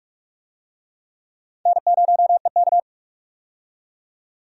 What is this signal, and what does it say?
Morse code, heard as a keyed tone, about 46 wpm, 700 Hz. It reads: N0EK
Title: USA Call Signs - 4 Characters - 45wpm